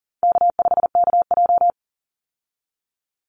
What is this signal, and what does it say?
Morse code, heard as a keyed tone, about 40 wpm, 700 Hz. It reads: K5KJ
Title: Call Signs - 40wpm